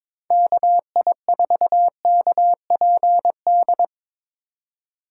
Morse code, heard as a keyed tone, about 23 wpm, 700 Hz. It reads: KI4KPD
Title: USA Call Signs - 6 Characters - 3x Character Spacing - 22wpm